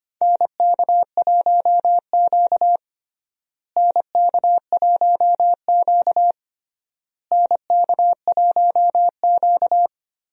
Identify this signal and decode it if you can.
Morse code, heard as a keyed tone, about 25 wpm, 700 Hz. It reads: NK1Q NK1Q NK1Q
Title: USA Call Signs - 4 Characters - Repeated 3x - 25wpm